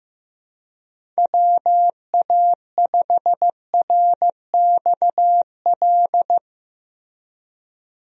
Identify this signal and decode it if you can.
Morse code, heard as a keyed tone, about 16 wpm, 700 Hz. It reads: WA5RXL
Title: USA Call Signs - 6 Characters - Repeated 3x - 15wpm